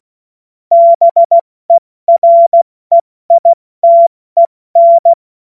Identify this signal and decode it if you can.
Morse code, heard as a keyed tone, about 16 wpm, spaced as for 14 wpm, 680 Hz. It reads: BEREITEN
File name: Morsekurs / Witze / Witz 14. März 2023 70 F80